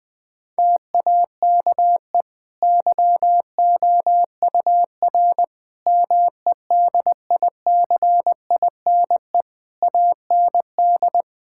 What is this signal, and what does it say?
Morse code, heard as a keyed tone, about 20 wpm, 700 Hz. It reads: TAKE YOUR MEDICINE AND
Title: Sentences from Top 1000 Words - Review Part 6 - 20wpm